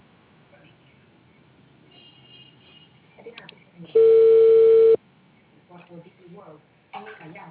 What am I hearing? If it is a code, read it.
Ringback tone (China, 450 Hz), with speech.